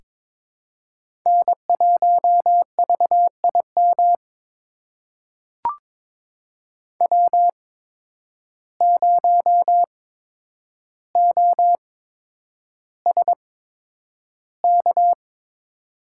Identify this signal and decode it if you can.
Morse code, heard as a keyed tone, about 21 wpm, spaced as for 9 wpm, 700 Hz. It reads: N1VIM W0OSK
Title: USA Call Signs - 5 Characters - 8x Character Spacing - 22wpm